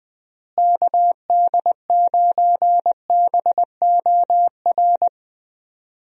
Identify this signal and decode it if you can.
Morse code, heard as a keyed tone, about 20 wpm, 700 Hz. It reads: KD9BOR
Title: USA Call Signs - 6 Characters - Repeated 3x - 20wpm